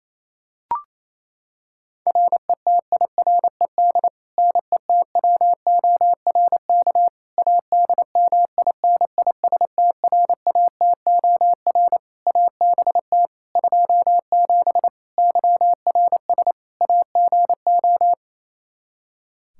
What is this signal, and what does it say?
Morse code, heard as a keyed tone, about 28 wpm, 700 Hz. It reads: RETIRED NETWORK ADMINISTRATOR ABT 27 YRS AGO
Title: K1USN SST QSO Practice 28wpm